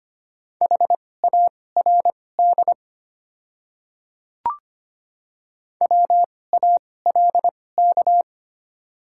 Morse code, heard as a keyed tone, about 25 wpm, spaced as for 18 wpm, 700 Hz. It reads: HARD WALK
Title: 2x Character Spacing - Top 500 Words - 25wpm